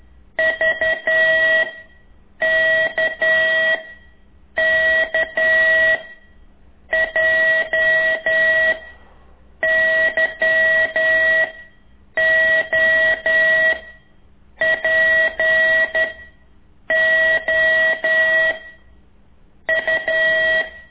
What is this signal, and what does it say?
Morse code, heard as a keyed tone, about 11 wpm, 1950 Hz. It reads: V K K J Y O P O U
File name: ik1weg2 / CORSI CW